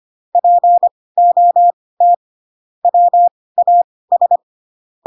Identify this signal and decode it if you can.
Morse code, heard as a keyed tone, about 25 wpm, spaced as for 18 wpm, 700 Hz. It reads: POT WAS
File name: Pinocchio / Pinocchio ch5 12 25 ik1hge